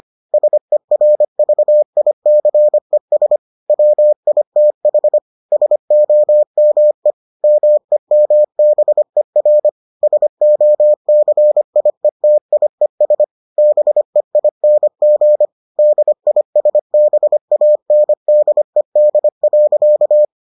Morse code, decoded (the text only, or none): SERVICES WITH SOME MEMBER SOCIETIES BEING DISBANDED.